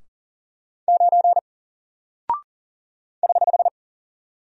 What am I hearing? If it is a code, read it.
Morse code, heard as a keyed tone, about 40 wpm, 700 Hz. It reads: 9 <HH>
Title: Single Character (Letter:Number:Punctuation:Prosign) Alt 40wpm